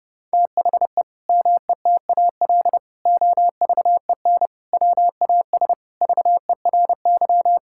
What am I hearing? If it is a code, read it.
Morse code, heard as a keyed tone, about 32 wpm, 700 Hz. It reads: THE METAL OVEN WAS VERY
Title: Sentences from Top 1000 Words - Review Part 6 - 30wpm